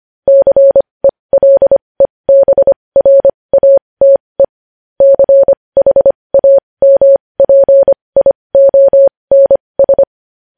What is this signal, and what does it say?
Morse code, heard as a keyed tone, about 25 wpm, spaced as for 20 wpm, 550 Hz. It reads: CELEBRATE CHAMPIONS
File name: podcast / News headlines-1x-25x20